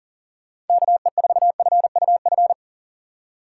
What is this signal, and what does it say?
Morse code, heard as a keyed tone, about 48 wpm, 700 Hz. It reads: KE4FUF
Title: Speed Racing - USA Call Signs 40wpm